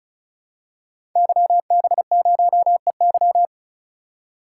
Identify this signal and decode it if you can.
Morse code, heard as a keyed tone, about 36 wpm, 700 Hz. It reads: YB0EY
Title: Code Groups - 5 Letter:Numbers - 35wpm